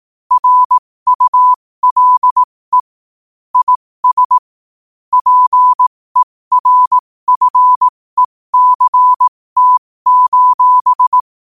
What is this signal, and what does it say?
Morse code, heard as a keyed tone, about 18 wpm, spaced as for 15 wpm, 1000 Hz. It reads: RULE IS PERFECT:
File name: qotdfiles / 15wpm